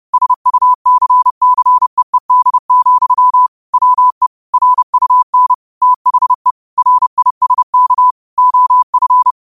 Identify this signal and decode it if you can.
Morse code, heard as a keyed tone, about 30 wpm, 1000 Hz. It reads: SUCCEED, WE RUN THE RISK OF
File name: qotdfiles / 30wpm